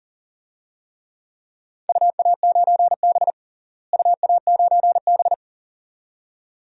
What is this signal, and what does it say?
Morse code, heard as a keyed tone, about 37 wpm, 700 Hz. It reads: UA9B UA9B
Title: DXCC-100 Call Signs - 4 Characters - Repeated 2x - 40wpm